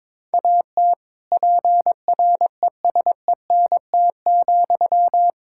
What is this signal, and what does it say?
Morse code, heard as a keyed tone, about 22 wpm, 700 Hz. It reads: AT PRESENT,
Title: Sentences from Top 1000 Words - Review Part 5 - 22wpm